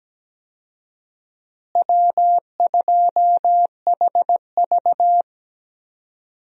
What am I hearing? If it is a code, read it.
Morse code, heard as a keyed tone, about 17 wpm, 700 Hz. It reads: W2HV
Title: USA Call Signs - 4 Characters - Repeated 3x - 17wpm